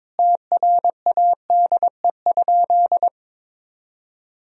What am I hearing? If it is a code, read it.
Morse code, heard as a keyed tone, about 22 wpm, 700 Hz. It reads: TRADE?
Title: Sentences from Top 1000 Words - Review Part 4 - 22wpm